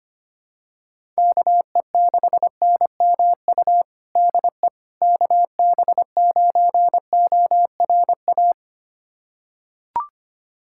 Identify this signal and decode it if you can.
Morse code, heard as a keyed tone, about 25 wpm, 700 Hz. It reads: KE6NMU DE KB9ORA
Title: USA Callsign DE Callsign 25wpm